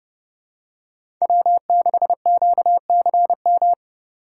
Morse code, heard as a keyed tone, about 30 wpm, 700 Hz. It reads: W6QCM
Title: USA Call Signs - 5 Characters - 3x Character Spacing - 30wpm